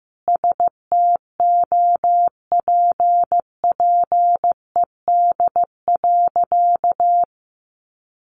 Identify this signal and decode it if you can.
Morse code, heard as a keyed tone, about 15 wpm, 700 Hz. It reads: STOPPED.
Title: A Sound Of Thunder 15wpm